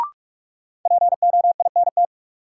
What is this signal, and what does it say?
Morse code, heard as a keyed tone, about 46 wpm, 700 Hz. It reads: POINT
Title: Sentences from Top 200 Words - New Word Drill - 45wpm